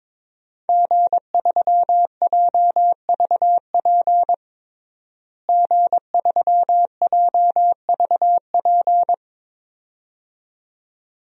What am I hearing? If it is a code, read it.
Morse code, heard as a keyed tone, about 22 wpm, 700 Hz. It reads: G3JVP G3JVP
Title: DXCC-100 Call Signs - 5 Characters - Repeated 2x - 22wpm